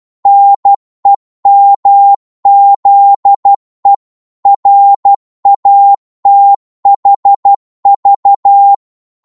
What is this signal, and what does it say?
Morse code, heard as a keyed tone, about 12 wpm, 800 Hz. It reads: NEMZE RATHV
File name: Morse / ltr-12wpm